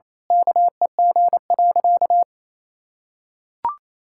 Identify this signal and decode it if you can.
Morse code, heard as a keyed tone, about 28 wpm, 700 Hz. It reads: KEG.
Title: Pangram Challenge v2 - 28wpm